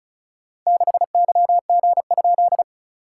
Morse code, heard as a keyed tone, about 36 wpm, 690 Hz. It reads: 6YG?
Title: SOTA QSO Practice 35wpm